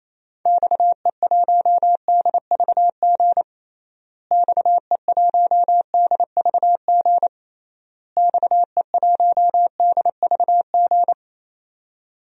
Morse code, heard as a keyed tone, about 27 wpm, 700 Hz. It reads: XE1DVG XE1DVG XE1DVG
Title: DXCC-100 Call Signs - 6 Characters - Repeated 3x - 28wpm